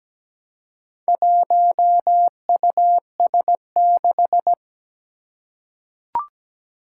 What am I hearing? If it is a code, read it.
Morse code, heard as a keyed tone, about 17 wpm, 700 Hz. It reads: 1US6
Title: Four Challenging Letter-Number Combinations 17wpm